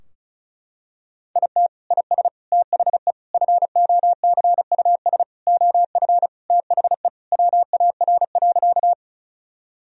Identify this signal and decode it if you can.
Morse code, heard as a keyed tone, about 36 wpm, 700 Hz. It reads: IT IS THE FOCUS OF THE WAR.
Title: Sentences from Top 1000 Words - Review Part 4 - 35wpm